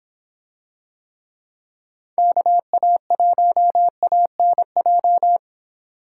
Morse code, heard as a keyed tone, about 26 wpm, 700 Hz. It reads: KA1ANJ